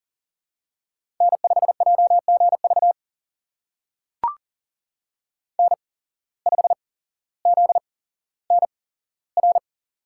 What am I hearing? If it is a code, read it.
Morse code, heard as a keyed tone, about 42 wpm, spaced as for 17 wpm, 700 Hz. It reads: N5JGV N5ZNR
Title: USA Call Signs - 5 Characters - 8x Character Spacing - 40wpm